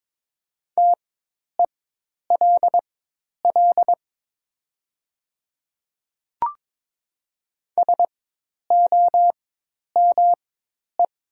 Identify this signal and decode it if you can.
Morse code, heard as a keyed tone, about 22 wpm, spaced as for 10 wpm, 700 Hz. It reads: TELL SOME
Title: 4x Character Spacing - Top 200 Words - 22wpm